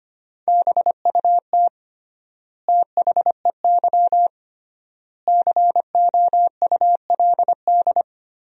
Morse code, heard as a keyed tone, about 25 wpm, 700 Hz. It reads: BUT THEY COULD
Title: 3x Word-Spacing - Sentences from Top 500 Words - 25wpm